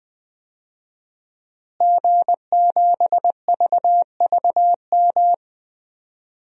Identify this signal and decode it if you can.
Morse code, heard as a keyed tone, about 20 wpm, 700 Hz. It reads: G7VVM
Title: DXCC-100 Call Signs - 5 Characters - Repeated 3x - 20wpm